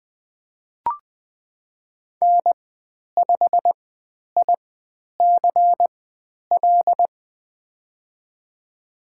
Morse code, heard as a keyed tone, about 20 wpm, spaced as for 10 wpm, 700 Hz. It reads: N5ICL
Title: 095 Lesson 16g - Letter c - Call Signs 20wpm